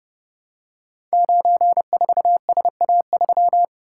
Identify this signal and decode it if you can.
Morse code, heard as a keyed tone, about 30 wpm, 700 Hz. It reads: 94SA3
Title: Code Groups - 5 Letter:Numbers - 30wpm